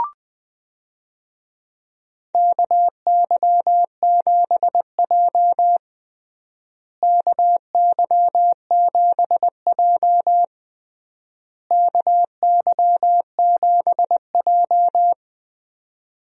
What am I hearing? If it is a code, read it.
Morse code, heard as a keyed tone, about 20 wpm, 700 Hz. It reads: KY7J KY7J KY7J
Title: USA Call Signs - 4 Characters - Repeated 3x - 20wpm